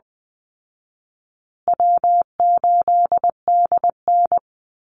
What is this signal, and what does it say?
Morse code, heard as a keyed tone, about 20 wpm, 700 Hz. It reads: W8DN
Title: Call Signs Repeated 2x - 20wpm